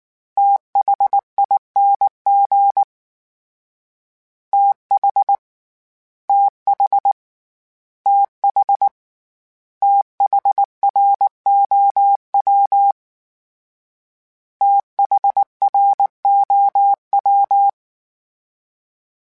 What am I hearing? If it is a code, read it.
Morse code, heard as a keyed tone, about 19 wpm, 800 Hz. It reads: THING TH TH TH THROW THROW
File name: TH / TH mp3s / TH Prefixes 3x2 @18